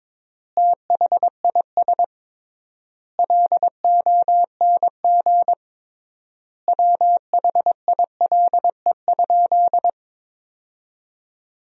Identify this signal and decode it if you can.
Morse code, heard as a keyed tone, about 22 wpm, 700 Hz. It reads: THIS LONG WHILE?
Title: 3x Word-Spacing - Sentences from Top 300 Words - 22wpm